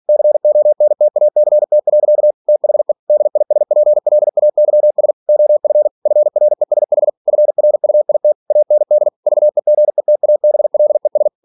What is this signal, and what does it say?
Morse code, heard as a keyed tone, about 46 wpm, 600 Hz. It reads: CONTACT. THE DISPLAYS OF FRESH FRUIT AND VEGETABLES
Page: Never 47wpm